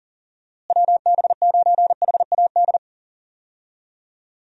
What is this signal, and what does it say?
Morse code, heard as a keyed tone, about 40 wpm, 700 Hz. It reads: WB9HAD
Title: USA Call Signs - 6 Characters - Repeated 3x - 40wpm